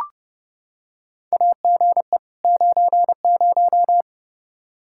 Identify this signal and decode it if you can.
Morse code, heard as a keyed tone, about 30 wpm, 700 Hz. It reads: AGE 90
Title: Practice QSO - Exchange 3 - More Info - 30wpm